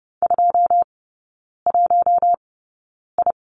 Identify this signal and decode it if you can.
Morse code, heard as a keyed tone, about 30 wpm, spaced as for 9 wpm, 700 Hz. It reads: 21I
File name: course / 30:10wpm / 087 Lesson 15e - Number 5 - Three Characters 30wpm